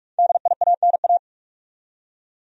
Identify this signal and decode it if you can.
Morse code, heard as a keyed tone, about 46 wpm, 700 Hz. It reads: DIANA
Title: USA Common Names - 45wpm